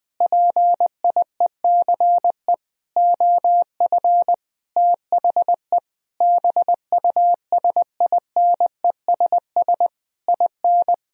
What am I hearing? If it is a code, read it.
Morse code, heard as a keyed tone, about 20 wpm, 700 Hz. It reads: PIECE OF THE BUSINESS IN
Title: Sentences from Top 1000 Words - Review Part 2 - 20wpm